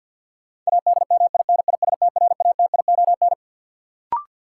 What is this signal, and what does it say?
Morse code, heard as a keyed tone, about 53 wpm, 700 Hz. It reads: ADMINISTRATION
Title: Sentences from Top 1300 Words - New Word Drill Alt - 50wpm